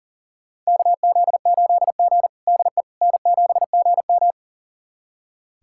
Morse code, heard as a keyed tone, about 37 wpm, 700 Hz. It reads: KZ8G DE N7GM
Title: USA Callsign DE Callsign - 4 Characters - 40wpm